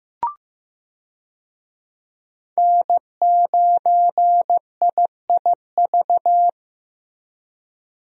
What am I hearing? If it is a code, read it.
Morse code, heard as a keyed tone, about 15 wpm, 700 Hz. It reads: N9IIV
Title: USA Call Signs - 5 Characters - Repeated 2x - 15wpm